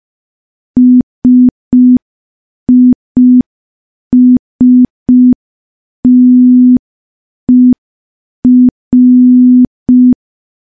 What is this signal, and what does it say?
Morse code, heard as a keyed tone, about 5 wpm, 250 Hz. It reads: SISTER
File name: morse / morse-training-05WPM